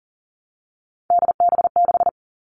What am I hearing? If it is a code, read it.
Morse code, heard as a keyed tone, about 40 wpm, 700 Hz. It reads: DB6
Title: All Cut Numbers - 3-Digit 40wpm